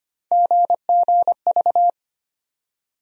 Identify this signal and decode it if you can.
Morse code, heard as a keyed tone, about 25 wpm, 700 Hz. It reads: GGV
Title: USA Call Sign Postfixes - 3 Letters 25wpm